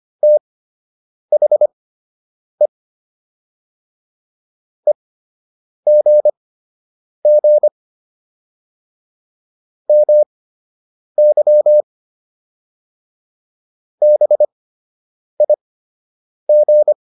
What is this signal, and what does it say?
Morse code, heard as a keyed tone, about 25 wpm, spaced as for 8 wpm, 600 Hz. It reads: THE EGG MY BIG